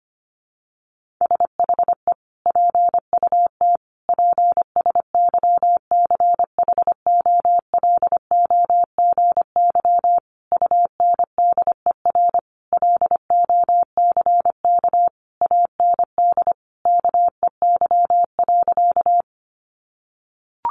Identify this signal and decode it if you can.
Morse code, heard as a keyed tone, about 25 wpm, 700 Hz. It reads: SHE PUT PSYCHOLOGY UNDER LOCK AND KEY.
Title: Sentences from Top 1400 Words 25wpm